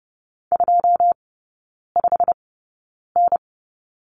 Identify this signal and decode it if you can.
Morse code, heard as a keyed tone, about 30 wpm, spaced as for 9 wpm, 700 Hz. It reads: 25N
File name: course / 30:10wpm / 087 Lesson 15e - Number 5 - Three Characters 30wpm